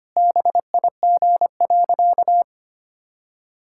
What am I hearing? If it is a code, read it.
Morse code, heard as a keyed tone, about 25 wpm, 700 Hz. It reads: BIG.
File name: practice / 25wpm / Sentences with 3-Word Phrases - 25wpm